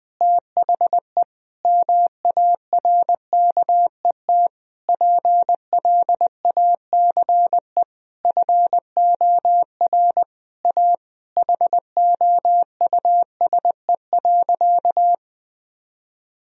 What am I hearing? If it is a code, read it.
Morse code, heard as a keyed tone, about 20 wpm, 700 Hz. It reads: THE MARKET PLACE FOR A HOUSE.
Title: Sentences from Top 1000 Words - Review Part 3 - 20wpm